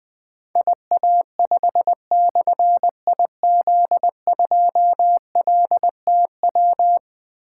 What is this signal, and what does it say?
Morse code, heard as a keyed tone, about 20 wpm, 700 Hz. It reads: IA5/IZ2LTW
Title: Difficult Call Signs 20wpm